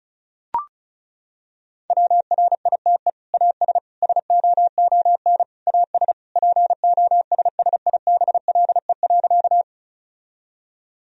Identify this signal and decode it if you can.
Morse code, heard as a keyed tone, about 36 wpm, 700 Hz. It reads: WRITE AS SOON AS POSSIBLE.
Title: Sentences from Top 1000 Words - Review Part 1 - 35wpm